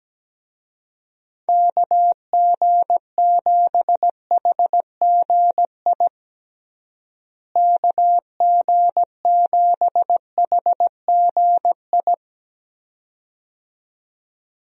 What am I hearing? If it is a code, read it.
Morse code, heard as a keyed tone, about 17 wpm, 700 Hz. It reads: KG7HGI KG7HGI
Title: USA Call Signs - 6 Characters - Repeated 2x - 17wpm